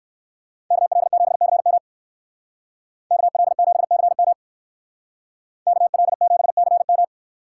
Morse code, heard as a keyed tone, about 58 wpm, 700 Hz. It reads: KL7CM KL7CM KL7CM